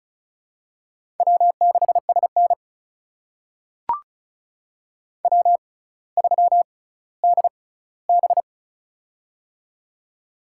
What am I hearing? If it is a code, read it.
Morse code, heard as a keyed tone, about 34 wpm, spaced as for 18 wpm, 700 Hz. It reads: W6SN W3DB
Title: USA Call Signs - 4 Characters - 6x Character Spacing - 35wpm